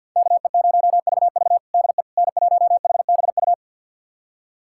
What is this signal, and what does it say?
Morse code, heard as a keyed tone, about 48 wpm, 700 Hz. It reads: KE0VV DE N1HBU
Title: USA Callsign DE Callsign - 5 Characters - 50wpm